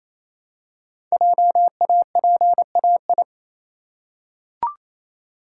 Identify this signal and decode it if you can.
Morse code, heard as a keyed tone, about 28 wpm, 700 Hz. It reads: JAPAI
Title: Code Groups - 5 Letter:Numbers - 28wpm